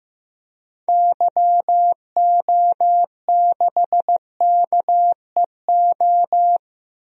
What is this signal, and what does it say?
Morse code, heard as a keyed tone, about 15 wpm, 700 Hz. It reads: YO6KEO
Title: DXCC-100 Call Signs - 6 Characters - 15wpm